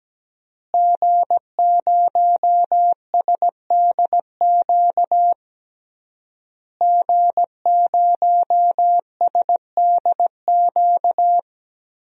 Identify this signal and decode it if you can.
Morse code, heard as a keyed tone, about 17 wpm, 700 Hz. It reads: G0SDQ G0SDQ
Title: DXCC-100 Call Signs - 5 Characters - Repeated 3x - 17wpm